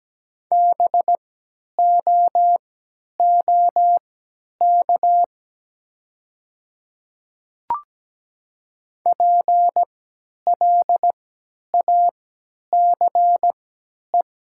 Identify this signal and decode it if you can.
Morse code, heard as a keyed tone, about 17 wpm, spaced as for 10 wpm, 700 Hz. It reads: BOOK PLACE
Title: 3x Character Spacing - Top 500 Words - 17wpm